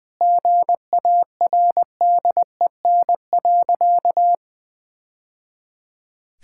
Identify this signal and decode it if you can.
Morse code, heard as a keyed tone, about 20 wpm, 700 Hz. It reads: GARDEN.
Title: Sentences with 3-Word Phrases - 20wpm